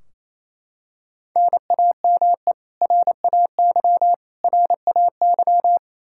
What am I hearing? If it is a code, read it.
Morse code, heard as a keyed tone, about 28 wpm, 700 Hz. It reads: NAME RAY RAY
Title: Practice QSO - Exchange 1 - Beginning - 28wpm